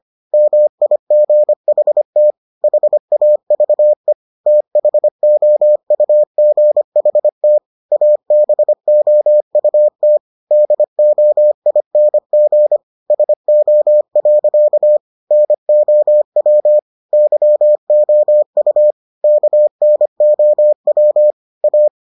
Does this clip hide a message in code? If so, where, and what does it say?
Yes: Morse code from 0.3 to 21.9 s, heard as a keyed tone, about 25 wpm, 600 Hz. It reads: MIGHT HAVE THOUGHT ABOUT DOING SO. NOW YOU KNOW A